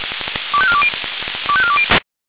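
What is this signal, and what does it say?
Five-tone selective call (ZVEI), heard as quick runs of five tones, tones of about 70 ms. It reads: 25420 25420
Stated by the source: ZVEI five-tone call 25419 received on an SDR.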